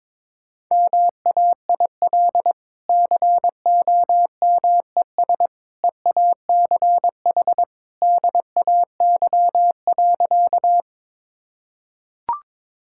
Morse code, heard as a keyed tone, about 22 wpm, 700 Hz. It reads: MAIL COMES EACH DAY.
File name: practice / 22wpm / Easy Sentences with Hints - 22wpm